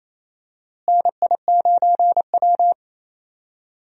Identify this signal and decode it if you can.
Morse code, heard as a keyed tone, about 28 wpm, 700 Hz. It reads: NI9W
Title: USA Call Signs - 4 Characters - Repeated 3x - 28wpm